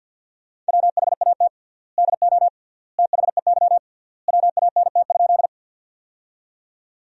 Morse code, heard as a keyed tone, about 50 wpm, 700 Hz. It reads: WHAT DO THEY WANT?
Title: 3x Word-Spacing - Sentences from Top 200 Words - 50wpm